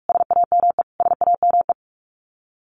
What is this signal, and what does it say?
Morse code, heard as a keyed tone, about 44 wpm, 700 Hz. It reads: SAME SAME
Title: Top 500 Words Repeated 45wpm